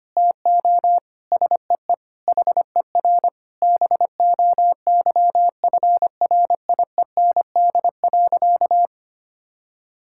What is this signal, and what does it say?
Morse code, heard as a keyed tone, about 25 wpm, 700 Hz. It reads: TO SEE HER BOYFRIEND.
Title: Sentences from Top 1000 Words - Review Part 6 - 25wpm